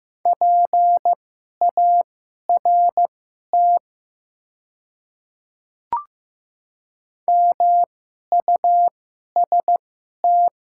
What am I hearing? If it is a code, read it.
Morse code, heard as a keyed tone, about 15 wpm, spaced as for 11 wpm, 700 Hz. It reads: PART MUST
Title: 2x Character Spacing - Top 300 Words - 15wpm